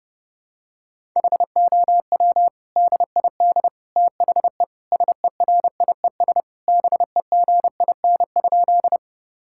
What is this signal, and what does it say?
Morse code, heard as a keyed tone, about 30 wpm, 700 Hz. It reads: HOW DID THE SERIES BEGIN?
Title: Sentences from Top 1000 Words - Review Part 5 - 30wpm